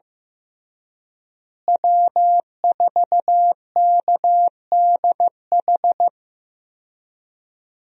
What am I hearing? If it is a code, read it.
Morse code, heard as a keyed tone, about 15 wpm, 700 Hz. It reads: W4KDH
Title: USA Call Signs - 5 Characters - Repeated 2x - 15wpm